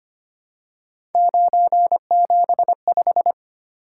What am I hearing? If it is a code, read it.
Morse code, heard as a keyed tone, about 25 wpm, 700 Hz. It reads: 975
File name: Three Letter-Number Combinations - 25wpm